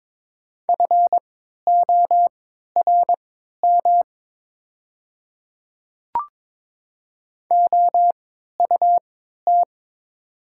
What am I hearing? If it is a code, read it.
Morse code, heard as a keyed tone, about 22 wpm, spaced as for 12 wpm, 700 Hz. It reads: FORM OUT
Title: 3x Character Spacing - Top 300 Words - 22wpm